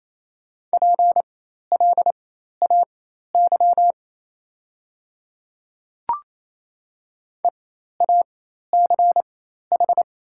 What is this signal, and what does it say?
Morse code, heard as a keyed tone, about 28 wpm, spaced as for 13 wpm, 690 Hz. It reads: PLAY EACH
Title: 4x Character Spacing - Top 200 Words - 28wpm